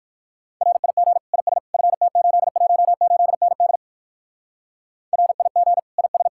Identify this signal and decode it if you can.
Morse code, heard as a keyed tone, about 56 wpm, 700 Hz. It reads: RIG IS FT818ND RIG IS
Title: Speed-Racing Common QSO Phrases - 45wpm